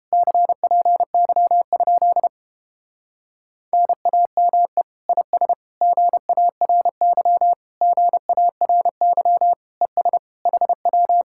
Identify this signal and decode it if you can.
Morse code, heard as a keyed tone, about 31 wpm, 700 Hz. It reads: CPY? NAME IS GARY GARY ES HW